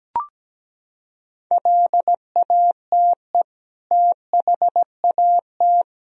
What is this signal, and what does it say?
Morse code, heard as a keyed tone, about 17 wpm, 700 Hz. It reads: LATE THAT
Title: Sentences from Top 1000 Words - Review Part 4 - 17wpm